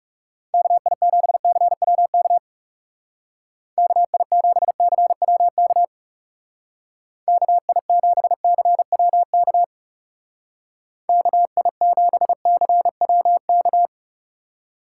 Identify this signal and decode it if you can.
Morse code, heard as a keyed tone, about 36 wpm, 700 Hz. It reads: KI7CWK KI7CWK KI7CWK KI7CWK
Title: Speed Racing - USA 2x3 Call Signs 30wpm